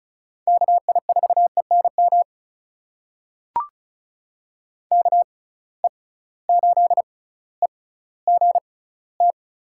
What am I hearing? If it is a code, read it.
Morse code, heard as a keyed tone, about 36 wpm, spaced as for 18 wpm, 700 Hz. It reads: KI4ENM KE8EGT
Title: USA Call Signs - 6 Characters - 6x Character Spacing - 35wpm